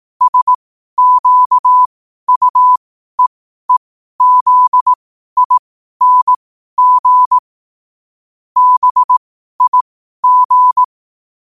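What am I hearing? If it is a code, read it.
Morse code, heard as a keyed tone, about 18 wpm, spaced as for 12 wpm, 1000 Hz. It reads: SQUEEZING BIG
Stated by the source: Morse at 12 wpm.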